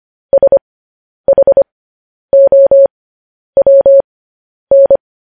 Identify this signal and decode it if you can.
Morse code, heard as a keyed tone, about 25 wpm, spaced as for 10 wpm, 550 Hz. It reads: SHOWN